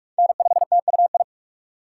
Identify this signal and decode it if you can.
Morse code, heard as a keyed tone, about 46 wpm, 700 Hz. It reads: N5TUI
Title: USA Call Signs - 2x Character Spacing - 45wpm